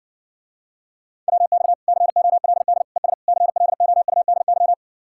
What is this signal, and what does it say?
Morse code, heard as a keyed tone, about 60 wpm, 700 Hz. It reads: WX COLD ES CLOUDY